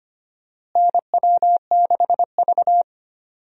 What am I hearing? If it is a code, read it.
Morse code, heard as a keyed tone, about 25 wpm, 700 Hz. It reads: NW6V
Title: USA Call Signs - 4 Characters - Repeated 3x - 25wpm